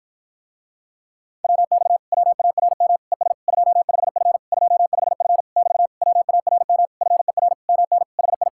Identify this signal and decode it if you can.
Morse code, heard as a keyed tone, about 52 wpm, 700 Hz. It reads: WX WARM ES 25F 25F <BT> WARM FER MN HI